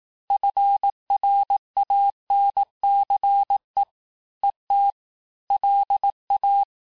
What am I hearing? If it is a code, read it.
Morse code, heard as a keyed tone, about 18 wpm, 800 Hz. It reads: FRANCE ET LA
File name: cours-du-lundi files / cours-1852-FIN-du-QSO-vitesse-17MPM